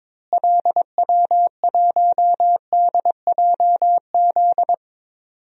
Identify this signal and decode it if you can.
Morse code, heard as a keyed tone, about 22 wpm, 700 Hz. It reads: LW1DJZ
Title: DXCC-100 Call Signs - 6 Characters - Repeated 2x - 22wpm